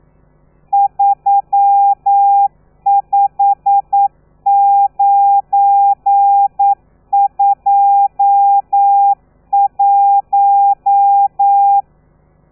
Morse code, heard as a keyed tone, about 9 wpm, 800 Hz. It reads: 35921
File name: telegrafia / i2tfj / Lez12 40